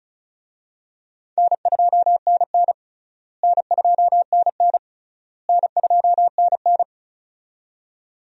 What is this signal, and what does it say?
Morse code, heard as a keyed tone, about 34 wpm, 700 Hz. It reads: N2NN N2NN N2NN
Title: USA Call Signs - 4 Characters - Repeated 3x - 35wpm